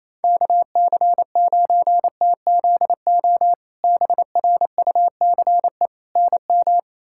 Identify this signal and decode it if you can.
Morse code, heard as a keyed tone, about 27 wpm, 700 Hz. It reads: KC9TZO BRUCE NM
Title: K1USN SST QSO Practice - 28wpm